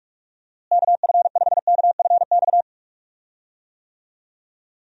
Morse code, heard as a keyed tone, about 46 wpm, 700 Hz. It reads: KF5KFX
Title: USA Call Signs - 6 Characters - Repeated 3x - 45wpm